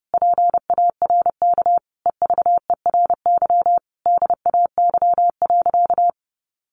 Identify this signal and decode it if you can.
Morse code, heard as a keyed tone, about 30 wpm, 700 Hz. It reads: PARK EVERY DAY.
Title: Sentences from Top 1300 Words 30wpm